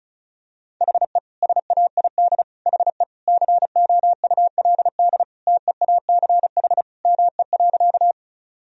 Morse code, heard as a keyed tone, about 34 wpm, 700 Hz. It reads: HE SAID HE COULD TEACH ME.
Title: Sentences from Top 1000 Words - Review Part 2 - 35wpm